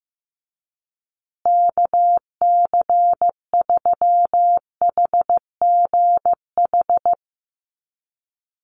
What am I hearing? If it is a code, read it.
Morse code, heard as a keyed tone, about 15 wpm, 700 Hz. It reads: KC3HGH
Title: USA Call Signs Repeated 2x 15wpm